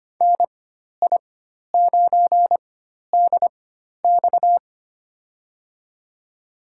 Morse code, heard as a keyed tone, about 25 wpm, spaced as for 12 wpm, 700 Hz. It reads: NI9DX
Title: USA Call Signs - 5 Characters - 4x Character Spacing - 25wpm